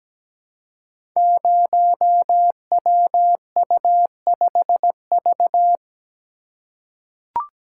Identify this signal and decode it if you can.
Morse code, heard as a keyed tone, about 17 wpm, 700 Hz. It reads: 0WU5V
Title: Code Groups - 5 Letter:Numbers - 17wpm